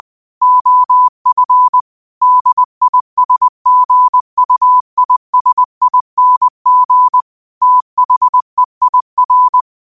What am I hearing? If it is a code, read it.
Morse code, heard as a keyed tone, about 20 wpm, 1000 Hz. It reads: OF DISGUISING THEIR
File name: qotdfiles / 20wpm